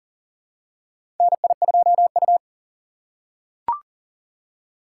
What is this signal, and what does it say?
Morse code, heard as a keyed tone, about 40 wpm, 700 Hz. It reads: NI2U
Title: USA Call Signs - 4 Characters - 3x Character Spacing - 40wpm